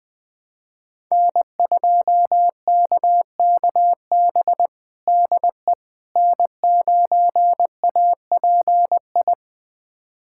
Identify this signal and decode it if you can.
Morse code, heard as a keyed tone, about 20 wpm, 700 Hz. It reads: N2KKB DE N9API
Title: USA Callsign DE Callsign - 5 Characters - 20wpm